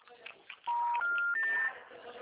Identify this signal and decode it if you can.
Special information tone, with noise.
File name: out- -2006- - - .32693